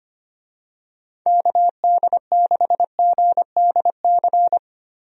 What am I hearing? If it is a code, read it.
Morse code, heard as a keyed tone, about 25 wpm, 700 Hz. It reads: KD6GDC
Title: USA Call Signs - 6 Characters - Repeated 2x - 25wpm